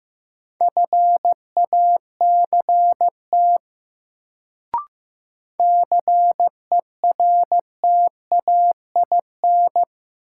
Morse code, heard as a keyed tone, about 15 wpm, 700 Hz. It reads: FACT CERTAIN
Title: Sentences from Top 500 Words - New Word Drill - 15wpm